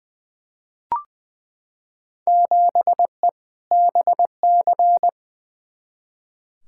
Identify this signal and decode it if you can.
Morse code, heard as a keyed tone, about 20 wpm, 700 Hz. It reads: 7E BC
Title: Field Day Exchanges Encore - 20wpm